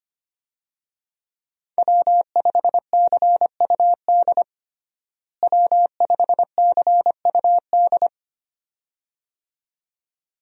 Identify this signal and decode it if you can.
Morse code, heard as a keyed tone, about 25 wpm, 700 Hz. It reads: W5CUD W5CUD
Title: USA Call Signs - 5 Characters - Repeated 2x - 25wpm